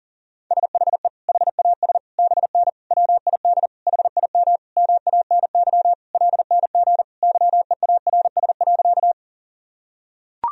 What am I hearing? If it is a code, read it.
Morse code, heard as a keyed tone, about 37 wpm, 700 Hz. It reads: SHE HAS BN WID HIM MANY LNG YEARS.
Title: Sentences from Top 200 Words and Abbreviations 40wpm